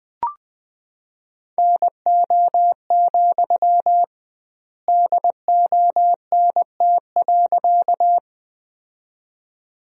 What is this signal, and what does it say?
Morse code, heard as a keyed tone, about 20 wpm, 700 Hz. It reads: NO, DONT.
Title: 2x Word-Spacing - Sentences from Top 300 Words - 20wpm